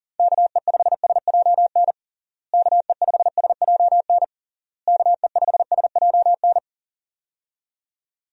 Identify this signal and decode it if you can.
Morse code, heard as a keyed tone, about 40 wpm, 700 Hz. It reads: KE5SJN KE5SJN KE5SJN
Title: USA Call Signs - 6 Characters - Repeated 3x - 40wpm